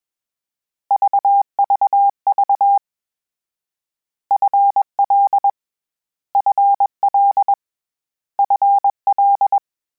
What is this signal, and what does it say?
Morse code, heard as a keyed tone, about 21 wpm, 800 Hz. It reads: VVV FL FL FL
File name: Combinations / Combinations-mp3s / Combinations@20